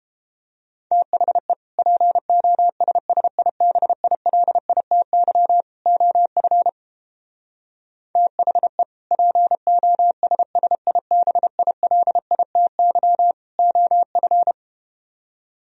Morse code, heard as a keyed tone, about 32 wpm, 700 Hz. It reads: THE POSSIBILITY OF THE POSSIBILITY OF